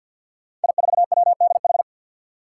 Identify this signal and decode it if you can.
Morse code, heard as a keyed tone, about 51 wpm, 700 Hz. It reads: I4WDH
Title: Five Challenging Letter-Number Combinations 50wpm